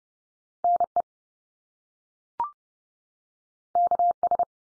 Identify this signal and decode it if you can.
Morse code, heard as a keyed tone, about 30 wpm, 700 Hz. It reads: NE KS
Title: US States 30WPM